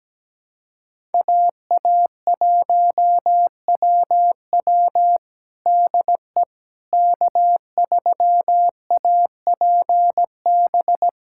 Morse code, heard as a keyed tone, about 17 wpm, 700 Hz. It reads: AA1WW DE K3APB
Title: USA Callsign DE Callsign - 5 Characters - 17wpm